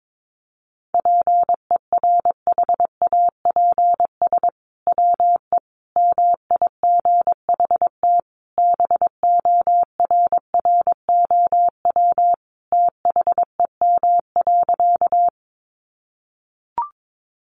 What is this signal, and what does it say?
Morse code, heard as a keyed tone, about 22 wpm, 700 Hz. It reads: PERHAPS WE MIGHT BORROW THEM.
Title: Sentences from Top 1500 Words 22wpm